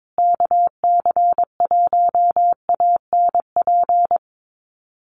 Morse code, heard as a keyed tone, about 22 wpm, 700 Hz. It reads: KC1ANP
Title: USA Call Signs Repeated 2x 22wpm